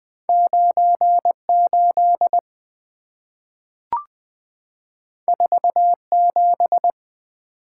Morse code, heard as a keyed tone, about 20 wpm, 690 Hz. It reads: 98 47
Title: Two Letter-Number Combinations - 20wpm